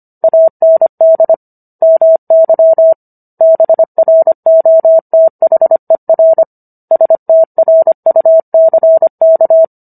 Morse code, heard as a keyed tone, about 25 wpm, 650 Hz. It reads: AND MY BROTHER STRUCK
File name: Book One / 16-Chapter 16 - 25 WPM